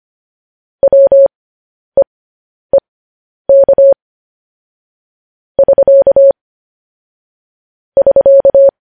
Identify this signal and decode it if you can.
Morse code, heard as a keyed tone, about 25 wpm, spaced as for 10 wpm, 550 Hz. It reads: WEEK <SK> <SK>
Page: news-headlines-in-morse-code-at-10-wpm